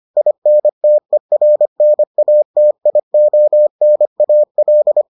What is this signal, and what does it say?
Morse code, heard as a keyed tone, about 25 wpm, 600 Hz. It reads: INTERNATIONAL